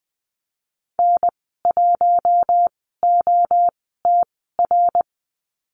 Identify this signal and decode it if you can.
Morse code, heard as a keyed tone, about 20 wpm, 700 Hz. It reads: N 1 O T R
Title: 076 Lesson 13g - Letter L - Call Signs 20wpm